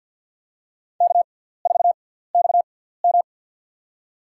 Morse code, heard as a keyed tone, about 53 wpm, spaced as for 17 wpm, 700 Hz. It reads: K4XM
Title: USA Call Signs - 4 Characters - 6x Character Spacing - 50wpm